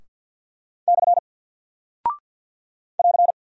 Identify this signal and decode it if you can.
Morse code, heard as a keyed tone, about 48 wpm, 700 Hz. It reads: / <AR>
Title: Single Character (Letter:Number:Punctuation:Prosign) 50wpm